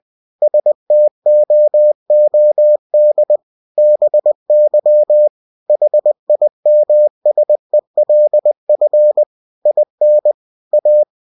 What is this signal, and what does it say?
Morse code, heard as a keyed tone, about 20 wpm, 600 Hz. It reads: STOOD BY HIMSELF IN A
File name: cw-books / winnie-the-pooh / winnie-20wpm-page-0026